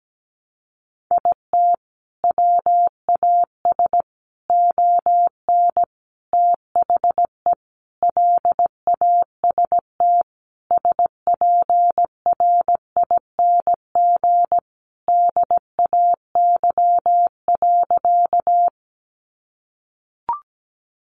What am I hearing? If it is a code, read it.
Morse code, heard as a keyed tone, about 17 wpm, 700 Hz. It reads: IT WAS ON THE LAST SPRING DAY.
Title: Sentences from Top 1200 Words 17wpm